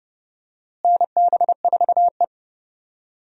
Morse code, heard as a keyed tone, about 30 wpm, 700 Hz. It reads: NB4E
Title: USA Call Signs - 4 Characters - Repeated 2x - 30wpm